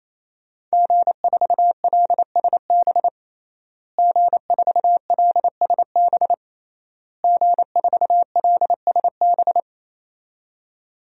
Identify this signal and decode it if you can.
Morse code, heard as a keyed tone, about 28 wpm, 700 Hz. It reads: G4LSB G4LSB G4LSB
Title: DXCC-100 Call Signs - 5 Characters - Repeated 3x - 28wpm